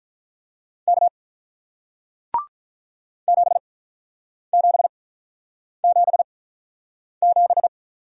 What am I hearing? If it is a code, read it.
Morse code, heard as a keyed tone, about 48 wpm, spaced as for 9 wpm, 700 Hz. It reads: K 7777